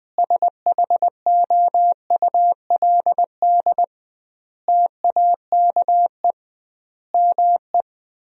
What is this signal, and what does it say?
Morse code, heard as a keyed tone, about 20 wpm, 700 Hz. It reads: SHOULD TAKE ME
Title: 2x Word-Spacing - Sentences from Top 300 Words - 20wpm